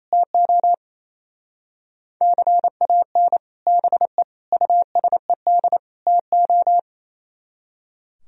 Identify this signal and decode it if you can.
Morse code, heard as a keyed tone, about 37 wpm, 700 Hz. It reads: TO CAN BE USED TO